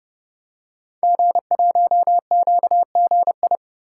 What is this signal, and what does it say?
Morse code, heard as a keyed tone, about 29 wpm, 700 Hz. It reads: G1QGI
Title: Code Groups - 5 Letter:Numbers - 30wpm